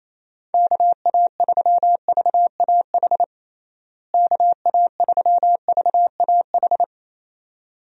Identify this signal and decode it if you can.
Morse code, heard as a keyed tone, about 27 wpm, 700 Hz. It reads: KA3VAH KA3VAH
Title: USA Call Signs - 6 Characters - Repeated 2x - 28wpm